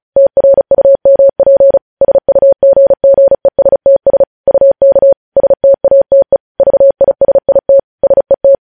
Morse code, heard as a keyed tone, about 36 wpm, 550 Hz. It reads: TRUMP SUGGESTS UK STATE VISIT SET
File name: podcast / News headlines-1x-35x35